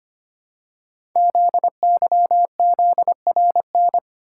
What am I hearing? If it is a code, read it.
Morse code, heard as a keyed tone, about 25 wpm, 700 Hz. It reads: ZYZRN
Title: Code Groups - 5 Letters - 25wpm